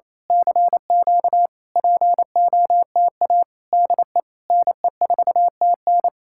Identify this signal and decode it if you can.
Morse code, heard as a keyed tone, about 27 wpm, 700 Hz. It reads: CQ POTA DE NE4TN
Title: Speed-Racing Common QSO Phrases - 28wpm